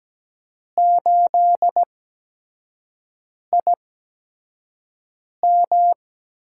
Morse code, heard as a keyed tone, about 17 wpm, spaced as for 5 wpm, 700 Hz. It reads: 8IM
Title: USA Call Signs - 5 Characters - 8x Character Spacing - 17wpm